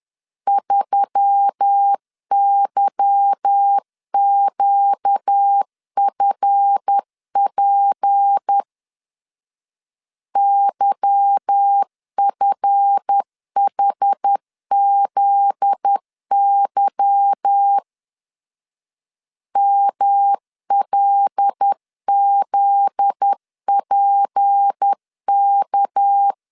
Morse code, heard as a keyed tone, about 10 wpm, 800 Hz. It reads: 3YQFP YFHZY MLZPK
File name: LEZIONI / 8 YJ45 / 8 Y 10wpm 2615